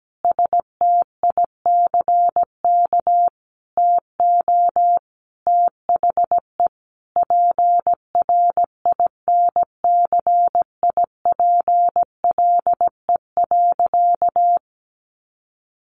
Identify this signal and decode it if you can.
Morse code, heard as a keyed tone, about 17 wpm, 700 Hz. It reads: STICK TO THE PRINCIPLE.
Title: Sentences from Top 1400 Words 17wpm